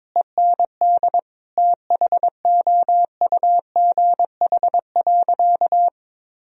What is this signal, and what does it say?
Morse code, heard as a keyed tone, about 22 wpm, 700 Hz. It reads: END THOUGH.
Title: Sentences from Top 1000 Words - Review Part 2 - 22wpm